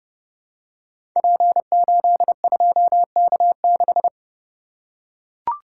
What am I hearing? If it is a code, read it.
Morse code, heard as a keyed tone, about 30 wpm, 700 Hz. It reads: P82K6
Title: Code Groups - 5 Letter:Numbers - 30wpm